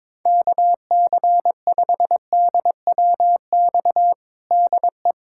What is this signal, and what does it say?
Morse code, heard as a keyed tone, about 22 wpm, 700 Hz. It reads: KC5DWX DE
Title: USA Callsign DE Callsign - 6 Characters - 22wpm